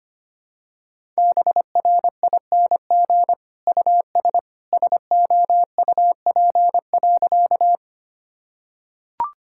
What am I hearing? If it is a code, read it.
Morse code, heard as a keyed tone, about 25 wpm, 700 Hz. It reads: BRING US SOUP.
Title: Sentences from Top 1000 Words - Review Part 6 - 25wpm